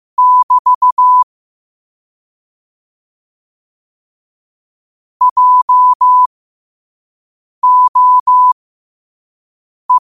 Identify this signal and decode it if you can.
Morse code, heard as a keyed tone, about 15 wpm, spaced as for 5 wpm, 1000 Hz. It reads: <BT> JOE
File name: qotdfiles / 5wpm